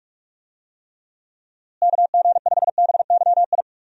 Morse code, heard as a keyed tone, about 45 wpm, 700 Hz. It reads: KG5BYI
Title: USA Call Signs - 6 Characters - Repeated 3x - 45wpm